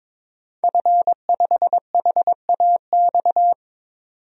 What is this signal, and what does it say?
Morse code, heard as a keyed tone, about 22 wpm, 700 Hz. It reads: F5HAX
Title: DXCC-100 Call Signs - 5 Characters - 22wpm